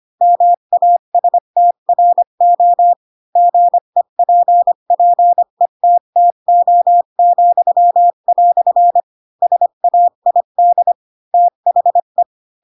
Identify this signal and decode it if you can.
Morse code, heard as a keyed tone, about 25 wpm, 700 Hz. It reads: MASTRO GEPPETTO," SAID THE